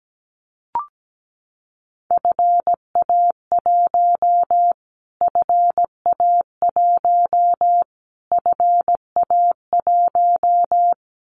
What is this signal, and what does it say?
Morse code, heard as a keyed tone, about 17 wpm, 700 Hz. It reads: FA1 FA1 FA1
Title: Callsign Prefixes for Top 25 DXCC - Repeated 3x - 17wpm